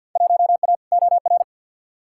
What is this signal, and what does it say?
Morse code, heard as a keyed tone, about 46 wpm, 700 Hz. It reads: 1A OR
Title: Field Day Exchanges - 50wpm